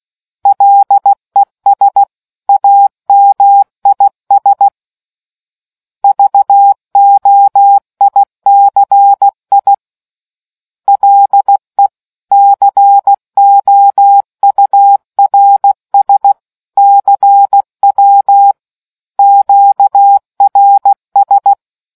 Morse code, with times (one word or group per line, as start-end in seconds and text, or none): 0.5-2.0 LES
2.5-4.7 AMIS
6.0-9.7 VOICI
10.9-11.9 LE
12.3-16.3 COURS
16.8-18.5 CW
19.2-21.5 QRS